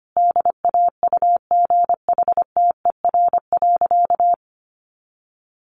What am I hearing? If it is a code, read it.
Morse code, heard as a keyed tone, about 25 wpm, 700 Hz. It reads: DAUGHTER.
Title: Sentences from Top 1500 Words 25wpm